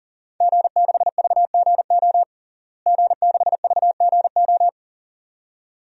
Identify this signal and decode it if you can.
Morse code, heard as a keyed tone, about 40 wpm, 700 Hz. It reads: G6VGO G6VGO
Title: DXCC-100 Call Signs - 5 Characters - Repeated 2x - 40wpm